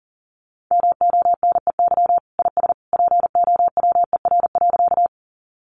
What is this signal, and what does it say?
Morse code, heard as a keyed tone, about 40 wpm, 700 Hz. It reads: MONEY IS POWER.
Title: Sentences from Top 500 Words 40wpm